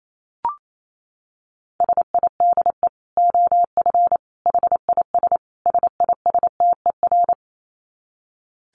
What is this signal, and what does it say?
Morse code, heard as a keyed tone, about 28 wpm, 700 Hz. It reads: SIDE OF HIS SISTER
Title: Dit Challenge 28wpm